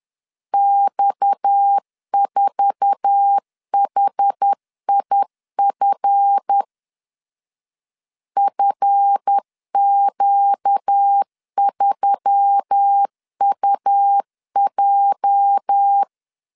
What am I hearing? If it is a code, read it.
Morse code, heard as a keyed tone, about 10 wpm, 800 Hz. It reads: X4HIF FQ3UJ
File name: LEZIONI / 8 YJ45 / 8 YJ4 10wpm 2633